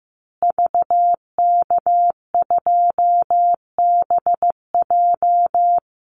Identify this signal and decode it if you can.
Morse code, heard as a keyed tone, about 15 wpm, 700 Hz. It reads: VK2BJ
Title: Call Signs Repeated 2x - 15wpm